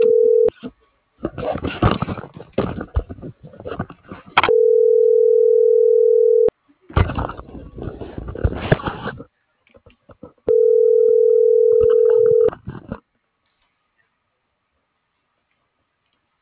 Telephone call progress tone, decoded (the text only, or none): ringback tone